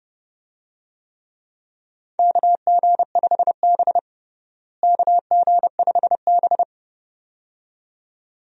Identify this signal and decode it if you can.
Morse code, heard as a keyed tone, about 30 wpm, 700 Hz. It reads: KG5B KG5B
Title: USA Call Signs - 4 Characters - Repeated 2x - 30wpm